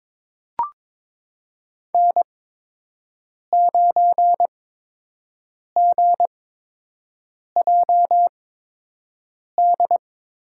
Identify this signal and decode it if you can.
Morse code, heard as a keyed tone, about 22 wpm, spaced as for 6 wpm, 700 Hz. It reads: N9GJD
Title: USA Call Signs - 8x Character Spacing - 22wpm